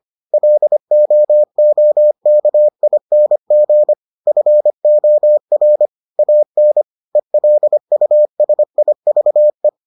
Morse code, heard as a keyed tone, about 25 wpm, 600 Hz. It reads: LOOKING FOR AN ELUSIVE